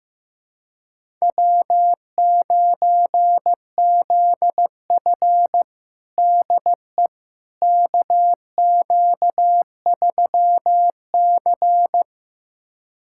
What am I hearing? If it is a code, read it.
Morse code, heard as a keyed tone, about 16 wpm, 700 Hz. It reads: W9ZF DE KQ3C
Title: USA Callsign DE Callsign - 4 Characters - 15wpm